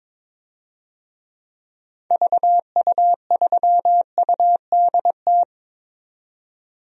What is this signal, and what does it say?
Morse code, heard as a keyed tone, about 22 wpm, 700 Hz. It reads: VU3UDT
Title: DXCC-100 Call Signs - 6 Characters - 22wpm